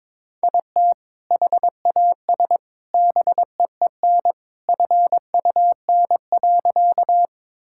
Morse code, heard as a keyed tone, about 22 wpm, 700 Hz. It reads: IT HAS BEEN FUN.
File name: practice / 22wpm / Sentences with 3-Word Phrases - 22wpm